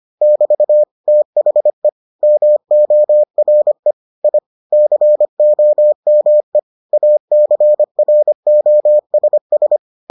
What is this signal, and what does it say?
Morse code, heard as a keyed tone, about 25 wpm, 600 Hz. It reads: <BT> THE MORE I COME ACROSS